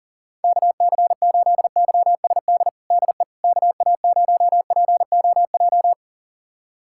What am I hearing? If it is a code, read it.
Morse code, heard as a keyed tone, about 40 wpm, 700 Hz. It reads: KC8YSD DE KA0POJ
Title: USA Callsign DE Callsign - 6 Characters - 40wpm